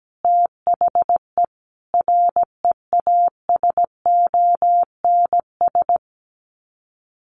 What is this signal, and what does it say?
Morse code, heard as a keyed tone, about 17 wpm, 700 Hz. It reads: THE REASONS
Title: Sets of 5 Words Encore2 17wpm